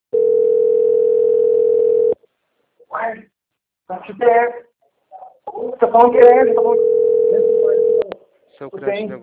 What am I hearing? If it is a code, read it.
Ringback tone (North America), 440 + 480 Hz, with speech.
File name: out- - - - - .981